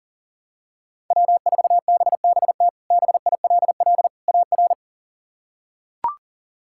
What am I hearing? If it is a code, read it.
Morse code, heard as a keyed tone, about 40 wpm, 700 Hz. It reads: W4BBT BILL AR
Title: K1USN SST QSO Practice - 40wpm